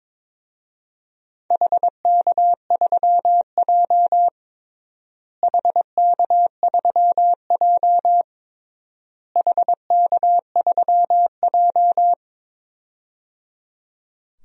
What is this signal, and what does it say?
Morse code, heard as a keyed tone, about 22 wpm, 700 Hz. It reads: HK3J HK3J HK3J
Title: DXCC-100 Call Signs - 4 Characters - Repeated 3x - 22wpm